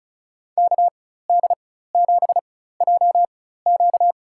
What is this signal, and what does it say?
Morse code, heard as a keyed tone, about 36 wpm, spaced as for 16 wpm, 700 Hz. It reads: KD7JQ
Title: USA Call Signs - 5 Characters - 4x Character Spacing - 35wpm